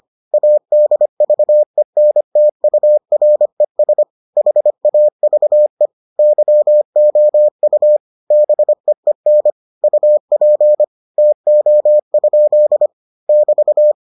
Morse code, heard as a keyed tone, about 25 wpm, 600 Hz. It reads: ADVENTURES HAVE YOU BEEN UP TO? <BT>